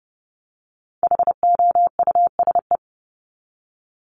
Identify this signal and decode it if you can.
Morse code, heard as a keyed tone, about 30 wpm, 700 Hz. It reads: HOUSE
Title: ING Suffix - Top 100 Words with Base Words 30wpm